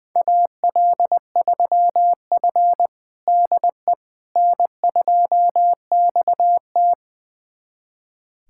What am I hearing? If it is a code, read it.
Morse code, heard as a keyed tone, about 20 wpm, 700 Hz. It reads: AL3F DE N2XT
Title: USA Callsign DE Callsign - 4 Characters - 20wpm